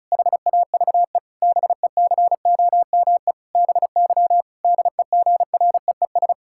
Morse code, heard as a keyed tone, about 36 wpm, 700 Hz. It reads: HAVE BECOME BY DEGREES
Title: Pangram Challenge v2 - 35wpm